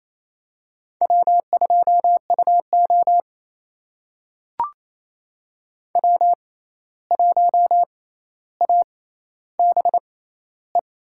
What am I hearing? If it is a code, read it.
Morse code, heard as a keyed tone, about 28 wpm, spaced as for 13 wpm, 700 Hz. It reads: W2UO W1ABE
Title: USA Call Signs - 6x Character Spacing - 28wpm